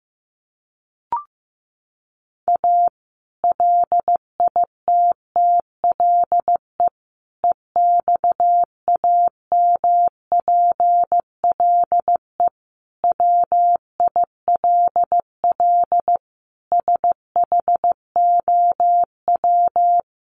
Morse code, heard as a keyed tone, about 15 wpm, 700 Hz. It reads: A LITTLE EXAMPLE WILL SHOW
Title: Sentences from Top 1500 Words 15wpm